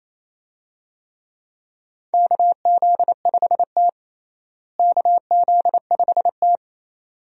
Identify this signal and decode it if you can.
Morse code, heard as a keyed tone, about 28 wpm, 700 Hz. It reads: KZ5T KZ5T
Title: USA Call Signs - 4 Characters - Repeated 2x - 28wpm